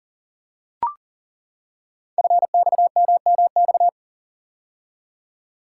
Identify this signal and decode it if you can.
Morse code, heard as a keyed tone, about 41 wpm, 700 Hz. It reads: FXMMX
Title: Code Groups - 5 Letters - 40wpm